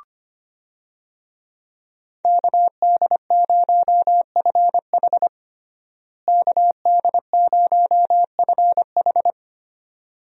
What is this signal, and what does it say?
Morse code, heard as a keyed tone, about 25 wpm, 700 Hz. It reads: KD0FH KD0FH
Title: USA Call Signs - 5 Characters - Repeated 2x - 25wpm